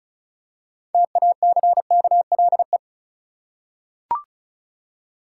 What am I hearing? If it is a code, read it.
Morse code, heard as a keyed tone, about 36 wpm, 700 Hz. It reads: TACKLE
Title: Six Letter Words - 35wpm